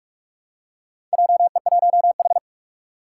Morse code, heard as a keyed tone, about 46 wpm, 700 Hz. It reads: JE1H
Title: Four Challenging Letter-Number Combinations 45wpm